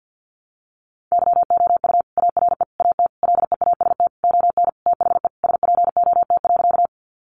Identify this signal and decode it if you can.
Morse code, heard as a keyed tone, about 49 wpm, 700 Hz. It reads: YOU ARE AT LEAST ON THE SPOT.
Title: Sentences from Top 1100 Words 50wpm